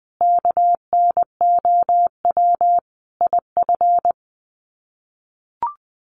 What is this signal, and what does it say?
Morse code, heard as a keyed tone, about 20 wpm, 700 Hz. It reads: KNOW IF
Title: Sets of 5 Words 20wpm